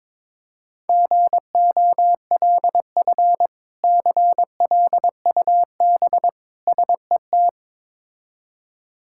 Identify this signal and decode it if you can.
Morse code, heard as a keyed tone, about 22 wpm, 700 Hz. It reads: GOLF CLUB SET
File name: Difficult Words with Hints - 22wpm